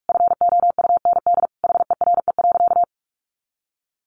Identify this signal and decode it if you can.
Morse code, heard as a keyed tone, about 44 wpm, 700 Hz. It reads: FOUND HERE.
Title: Sentences from Top 1500 Words 45wpm